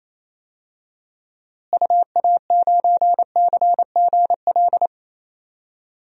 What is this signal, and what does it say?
Morse code, heard as a keyed tone, about 27 wpm, 700 Hz. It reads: UA9CGL
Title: DXCC-100 Call Signs - 6 Characters - Repeated 3x - 28wpm